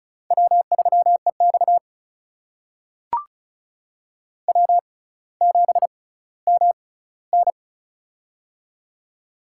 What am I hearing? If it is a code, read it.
Morse code, heard as a keyed tone, about 36 wpm, spaced as for 18 wpm, 700 Hz. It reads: W3EX W7MN
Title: USA Call Signs - 4 Characters - 6x Character Spacing - 35wpm